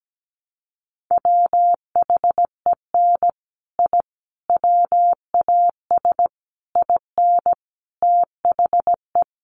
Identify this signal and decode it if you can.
Morse code, heard as a keyed tone, about 17 wpm, 700 Hz. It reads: WHEN I WAS IN THE
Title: Sets of 5 Words Encore2 17wpm